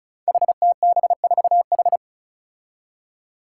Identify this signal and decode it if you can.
Morse code, heard as a keyed tone, about 36 wpm, 700 Hz. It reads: HTB4H
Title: Five Challenging Letter-Number Combinations 35wpm